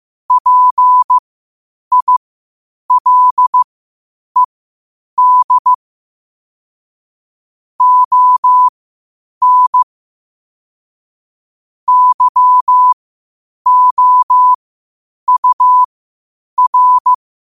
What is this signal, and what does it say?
Morse code, heard as a keyed tone, about 15 wpm, spaced as for 8 wpm, 1000 Hz. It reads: PILED ON YOUR
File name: qotdfiles / 8wpm